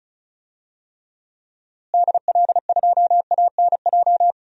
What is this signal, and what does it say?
Morse code, heard as a keyed tone, about 36 wpm, 700 Hz. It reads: DL2ANJ
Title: DXCC-100 Call Signs - 6 Characters - Repeated 3x - 35wpm